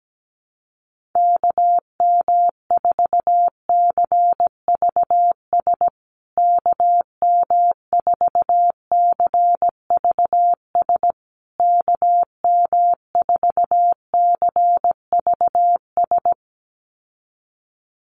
Morse code, heard as a keyed tone, about 17 wpm, 700 Hz. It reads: KM4CVS KM4CVS KM4CVS
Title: USA Call Signs Repeated 3x 17wpm